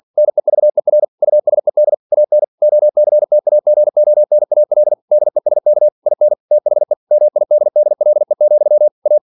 Morse code, heard as a keyed tone, about 46 wpm, 600 Hz. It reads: NEVER USED AN OCTAGONAL DESK IN THE MIDDLE, A
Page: Never 47wpm